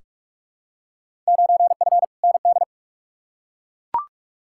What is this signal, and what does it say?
Morse code, heard as a keyed tone, about 42 wpm, 700 Hz. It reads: 9F ND
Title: Field Day Exchanges Encore - 45wpm